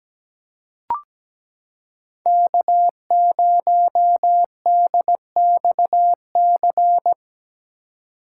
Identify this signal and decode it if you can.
Morse code, heard as a keyed tone, about 17 wpm, 700 Hz. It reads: K0DXC
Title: K1USN SST QSO Practice - 17wpm